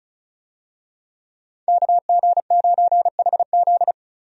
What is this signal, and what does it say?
Morse code, heard as a keyed tone, about 36 wpm, 700 Hz. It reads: KG9HZ
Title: USA Call Signs - 5 Characters - 35wpm